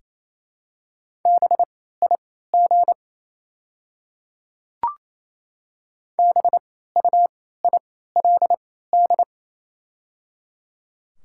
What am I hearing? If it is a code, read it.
Morse code, heard as a keyed tone, about 28 wpm, spaced as for 16 wpm, 700 Hz. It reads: BIG BUILD
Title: 3x Character Spacing - Top 500 Words - 28wpm